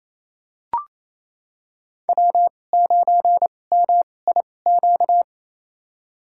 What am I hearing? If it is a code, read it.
Morse code, heard as a keyed tone, about 28 wpm, spaced as for 20 wpm, 700 Hz. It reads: W9MIQ
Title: USA Call Signs - 5 Characters - 2x Character Spacing - 28wpm